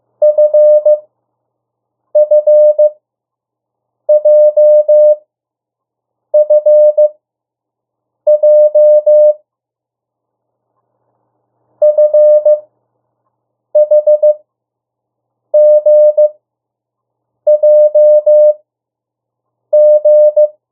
Morse code, heard as a keyed tone, about 15 wpm, spaced as for 6 wpm, 600 Hz. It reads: FFJFJ FHGJG
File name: morse mp3 / 02 urok 1